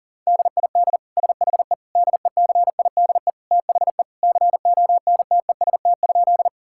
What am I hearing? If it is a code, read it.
Morse code, heard as a keyed tone, about 40 wpm, 700 Hz. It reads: DID SHE DECIDE THE CONTEST?
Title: Sentences from Top 1600 Words 40wpm